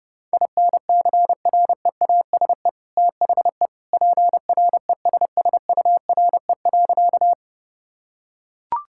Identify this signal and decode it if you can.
Morse code, heard as a keyed tone, about 30 wpm, 700 Hz. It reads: INCREASE THE PRESSURE.
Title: Sentences from Top 1000 Words - Review Part 6 - 30wpm